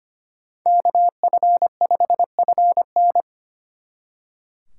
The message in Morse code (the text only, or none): KF5FN